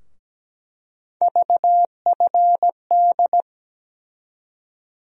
Morse code, heard as a keyed tone, about 17 wpm, 700 Hz. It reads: VFD
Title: USA Call Sign Postfixes - 3 Letters 17wpm